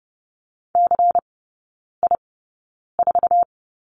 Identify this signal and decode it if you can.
Morse code, heard as a keyed tone, about 30 wpm, spaced as for 9 wpm, 690 Hz. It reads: CI4
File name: Lesson 16e - Letter c - Three Characters 30wpm